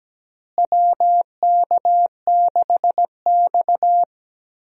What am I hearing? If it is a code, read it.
Morse code, heard as a keyed tone, about 17 wpm, 700 Hz. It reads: WK6X
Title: USA Call Signs - 4 Characters - Repeated 2x - 17wpm